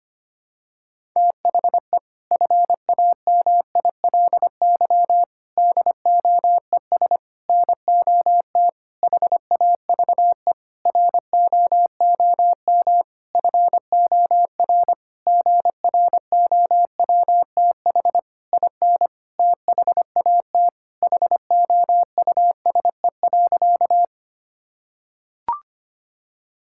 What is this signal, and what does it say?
Morse code, heard as a keyed tone, about 25 wpm, 700 Hz. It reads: THE FAMILY DOES NOT HAVE ROOM FOR GROWTH IN THAT HOUSE.
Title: Sentences from Top 1000 Words - Review Part 5 - 25wpm